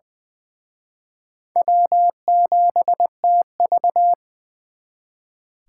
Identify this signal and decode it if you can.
Morse code, heard as a keyed tone, about 23 wpm, 700 Hz. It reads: W7TV
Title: Speed Racing - USA 1x2 Call Signs 20wpm